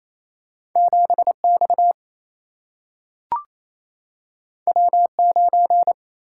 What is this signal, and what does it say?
Morse code, heard as a keyed tone, about 28 wpm, 700 Hz. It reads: 7 X W 9
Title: Single-Letter Single-Number 28wpm